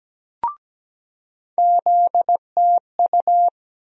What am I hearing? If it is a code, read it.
Morse code, heard as a keyed tone, about 17 wpm, 700 Hz. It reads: ZTU
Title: USA Call Sign Postfixes - 3 Letters 17wpm